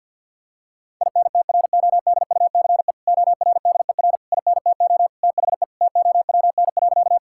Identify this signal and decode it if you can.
Morse code, heard as a keyed tone, about 50 wpm, 700 Hz. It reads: INTRODUCE ORDER INTO THE TOWN.
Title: Sentences from Top 1600 Words 50wpm